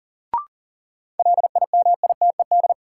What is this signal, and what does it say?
Morse code, heard as a keyed tone, about 40 wpm, 700 Hz. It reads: LIMITED
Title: Sentences from Top 900 Words - New Word Drill - 40wpm